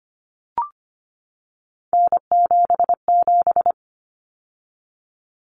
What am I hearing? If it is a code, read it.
Morse code, heard as a keyed tone, about 25 wpm, 700 Hz. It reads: N77
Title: Common Cut Numbers - 3-Digit 25wpm